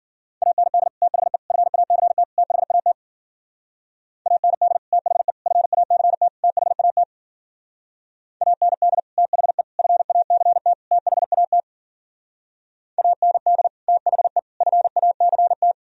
Morse code, heard as a keyed tone, about 57 wpm, 700 Hz. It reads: AND THE FACT THAT AND THE FACT THAT AND THE FACT THAT AND THE FACT